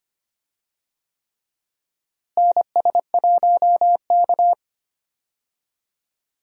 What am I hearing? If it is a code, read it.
Morse code, heard as a keyed tone, about 25 wpm, 700 Hz. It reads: NS1K
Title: USA Call Signs - 4 Characters - Repeated 3x - 25wpm